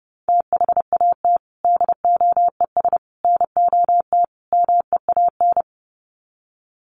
Morse code, heard as a keyed tone, about 30 wpm, 700 Hz. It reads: THAT DOES NOT MEAN
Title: Sets of 4 Words 30wpm